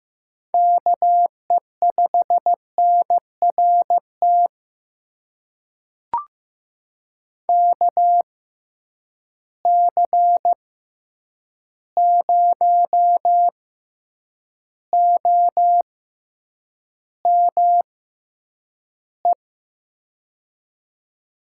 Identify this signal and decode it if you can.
Morse code, heard as a keyed tone, about 15 wpm, spaced as for 8 wpm, 700 Hz. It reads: KE5NRT KC0OME
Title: USA Call Signs - 6 Characters - 6x Character Spacing - 15wpm